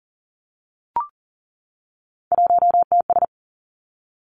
Morse code, heard as a keyed tone, about 42 wpm, 700 Hz. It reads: 1TS
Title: Three Challenging Letter-Number Combinations 40wpm